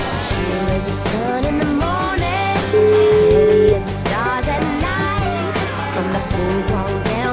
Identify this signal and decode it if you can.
Ringback tone (Europe, 425 Hz), with noise.